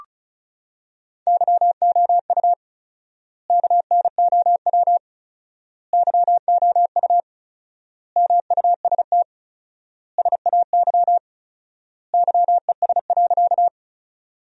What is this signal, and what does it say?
Morse code, heard as a keyed tone, about 36 wpm, 700 Hz. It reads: YOU KNOW YOU MUST SAY YES.
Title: 4x Word-Spacing - Sentences from Top 500 Words - 35wpm